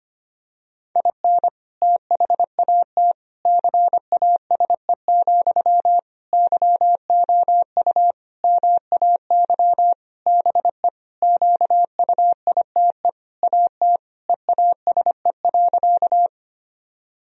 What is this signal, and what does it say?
Morse code, heard as a keyed tone, about 25 wpm, 700 Hz. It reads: IN THAT CASE, YOU MAY BE QUITE AT EASE.
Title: Sentences from Top 1000 Words - Review Part 1 - 25wpm